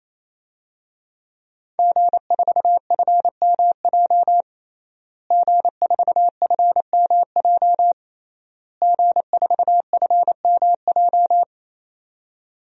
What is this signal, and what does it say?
Morse code, heard as a keyed tone, about 28 wpm, 700 Hz. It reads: G4FMJ G4FMJ G4FMJ
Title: DXCC-100 Call Signs - 5 Characters - Repeated 3x - 28wpm